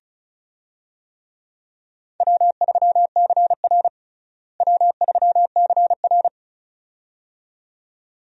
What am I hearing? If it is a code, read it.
Morse code, heard as a keyed tone, about 36 wpm, 700 Hz. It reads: W3CR W3CR
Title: USA Call Signs - 4 Characters - Repeated 2x - 35wpm